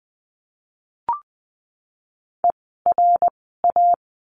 Morse code, heard as a keyed tone, about 20 wpm, 700 Hz. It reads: E R A
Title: 054 Lesson 10f - Letter r - Words 20wpm